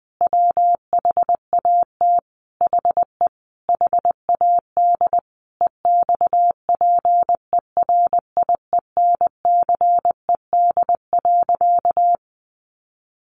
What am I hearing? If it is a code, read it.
Morse code, heard as a keyed tone, about 20 wpm, 700 Hz. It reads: WHAT HE HAD EXPERIENCED.
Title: Sentences from Top 1100 Words 20wpm